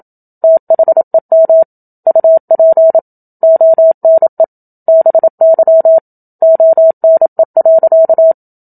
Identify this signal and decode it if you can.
Morse code, heard as a keyed tone, about 27 wpm, 650 Hz. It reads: THEM UP ONE BY ONE.
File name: Book Two / 01-Chapter 1 - 27 WPM